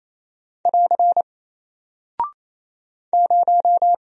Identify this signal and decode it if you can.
Morse code, heard as a keyed tone, about 27 wpm, 700 Hz. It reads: <AR> 0
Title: Single Character (Letter:Number:Punctuation:Prosign) Alt 28wpm